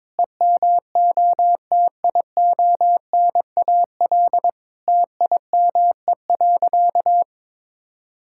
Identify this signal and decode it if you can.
Morse code, heard as a keyed tone, about 22 wpm, 700 Hz. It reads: EMOTIONAL TIME.
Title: Sentences from Top 1000 Words - Review Part 5 - 22wpm